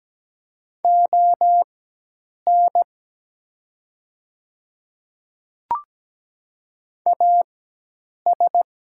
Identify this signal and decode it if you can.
Morse code, heard as a keyed tone, about 17 wpm, spaced as for 8 wpm, 700 Hz. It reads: ON AS
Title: 4x Character Spacing - Top 200 Words - 17wpm